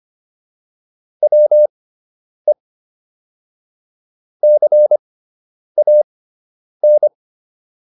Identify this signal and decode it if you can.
Morse code, heard as a keyed tone, about 25 wpm, spaced as for 9 wpm, 600 Hz. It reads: WE CAN